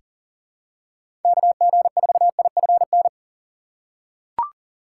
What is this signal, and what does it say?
Morse code, heard as a keyed tone, about 40 wpm, 700 Hz. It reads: KG4IFN
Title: USA Call Signs - 3x Character Spacing - 40wpm